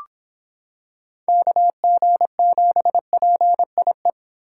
Morse code, heard as a keyed tone, about 25 wpm, 700 Hz. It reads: KG7PIE